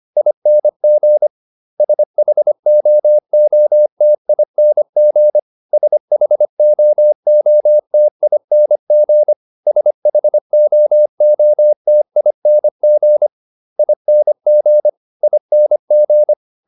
Morse code, heard as a keyed tone, about 25 wpm, 600 Hz. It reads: ING SHOOTING SHOOTING SHOOTING ING ING